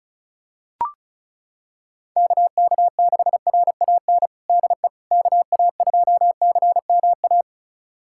Morse code, heard as a keyed tone, about 36 wpm, 700 Hz. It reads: KK6RAN DE KA2CMA
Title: USA Callsign DE Callsign - 6 Characters - 35wpm